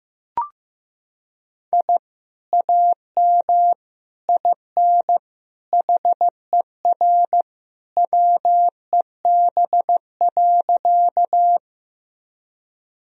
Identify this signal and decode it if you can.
Morse code, heard as a keyed tone, about 15 wpm, 700 Hz. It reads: I AM IN HER WEB.
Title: Sentences from Top 1000 Words - Review Part 4 - 15wpm